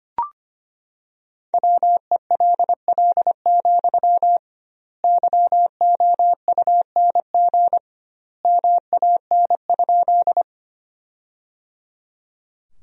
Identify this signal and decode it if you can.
Morse code, heard as a keyed tone, about 25 wpm, 700 Hz. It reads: WELL, YOUNG MAN?
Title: 2x Word-Spacing - Sentences from Top 400 Words - 25wpm